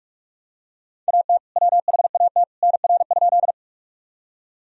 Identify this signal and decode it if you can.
Morse code, heard as a keyed tone, about 46 wpm, 700 Hz. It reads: AT WHAT NR?
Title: Sentences from Top 100 Words and Abbreviations 45wpm